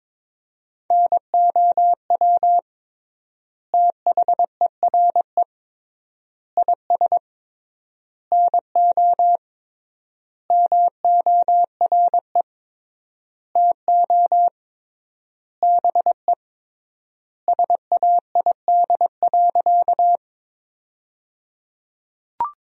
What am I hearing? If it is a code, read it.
Morse code, heard as a keyed tone, about 22 wpm, 700 Hz. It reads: NOW THERE IS NO MORE TO BE SAID.
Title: 3x Word-Spacing - Sentences from Top 100 Words - 22wpm